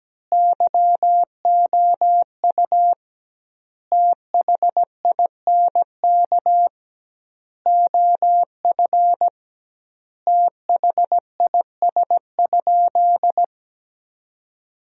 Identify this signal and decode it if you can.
Morse code, heard as a keyed tone, about 17 wpm, 700 Hz. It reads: YOU THINK OF THIS?
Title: 2x Word-Spacing - Sentences from Top 200 Words - 17wpm